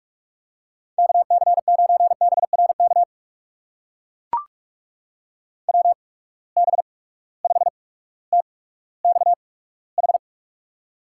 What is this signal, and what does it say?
Morse code, heard as a keyed tone, about 44 wpm, spaced as for 19 wpm, 700 Hz. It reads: KC9BRK WB5TXH
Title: USA Call Signs - 6 Characters - 8x Character Spacing - 45wpm